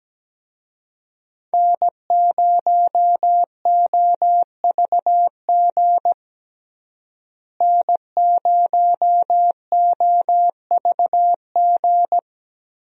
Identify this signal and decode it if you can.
Morse code, heard as a keyed tone, about 17 wpm, 700 Hz. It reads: N0OVG N0OVG
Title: USA Call Signs - 5 Characters - Repeated 3x - 17wpm